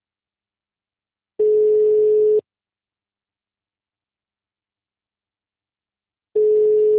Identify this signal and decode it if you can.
Ringback tone (Europe, 425 Hz).